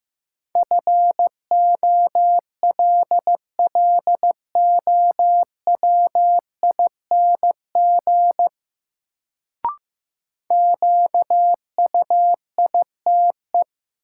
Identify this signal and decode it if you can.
Morse code, heard as a keyed tone, about 16 wpm, 700 Hz. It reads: FOLLOWING QUITE
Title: Sentences from Top 700 Words - New Word Drill - 15wpm